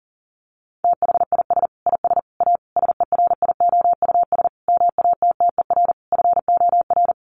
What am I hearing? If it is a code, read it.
Morse code, heard as a keyed tone, about 40 wpm, 700 Hz. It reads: THIS IS A SERIOUS MATTER FOR
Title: Sentences from Top 1100 Words 40wpm